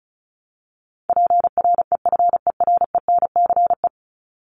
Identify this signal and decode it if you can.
Morse code, heard as a keyed tone, about 34 wpm, 700 Hz. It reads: PREFERENCE
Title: Top 1251-1500 Words 35wpm